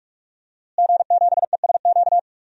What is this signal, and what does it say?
Morse code, heard as a keyed tone, about 46 wpm, 700 Hz. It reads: G7ESQ
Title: DXCC-100 Call Signs - 5 Characters - Repeated 3x - 45wpm